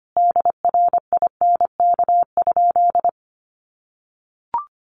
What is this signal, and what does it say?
Morse code, heard as a keyed tone, about 25 wpm, 700 Hz. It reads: DRINK?
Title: Sentences from Top 1500 Words 25wpm